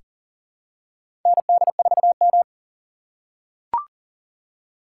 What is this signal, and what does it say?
Morse code, heard as a keyed tone, about 40 wpm, 700 Hz. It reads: ND4M
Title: USA Call Signs - 3x Character Spacing - 40wpm